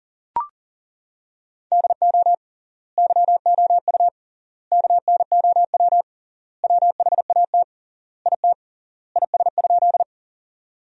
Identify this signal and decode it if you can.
Morse code, heard as a keyed tone, about 40 wpm, 700 Hz. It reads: DO YOU KNOW WHAT IT IS?
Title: 3x Word-Spacing - Sentences from Top 100 Words - 40wpm